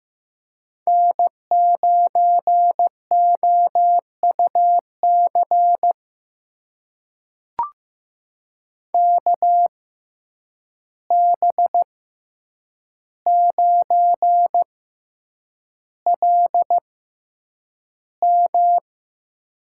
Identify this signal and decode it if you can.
Morse code, heard as a keyed tone, about 15 wpm, spaced as for 8 wpm, 700 Hz. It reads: N9OUC KB9LM
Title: USA Call Signs - 5 Characters - 6x Character Spacing - 15wpm